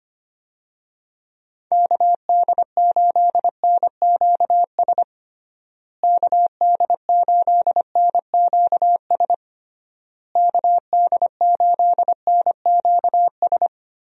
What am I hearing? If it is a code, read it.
Morse code, heard as a keyed tone, about 25 wpm, 700 Hz. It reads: KD8NQS KD8NQS KD8NQS
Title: USA Call Signs - 6 Characters - Repeated 3x - 25wpm